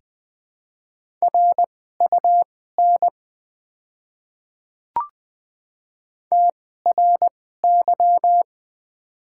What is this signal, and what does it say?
Morse code, heard as a keyed tone, about 20 wpm, spaced as for 14 wpm, 700 Hz. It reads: RUN TRY
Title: 2x Character Spacing - Top 300 Words - 20wpm